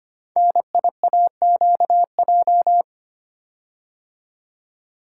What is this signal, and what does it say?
Morse code, heard as a keyed tone, about 25 wpm, 700 Hz. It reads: NIAQJ
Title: Code Groups - 5 Letters - 25wpm